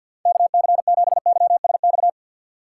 Morse code, heard as a keyed tone, about 47 wpm, 700 Hz. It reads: KC7YSX
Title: USA Call Signs - 6 Characters - 6x Character Spacing - 50wpm